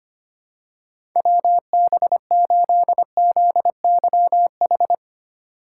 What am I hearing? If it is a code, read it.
Morse code, heard as a keyed tone, about 25 wpm, 700 Hz. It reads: WB8ZYH
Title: USA Call Signs - 6 Characters - 4x Character Spacing - 25wpm